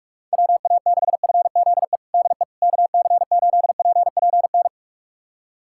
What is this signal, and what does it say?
Morse code, heard as a keyed tone, about 46 wpm, 700 Hz. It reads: WA6FZE DE KC8PPN
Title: USA Callsign DE Callsign - 6 Characters - 45wpm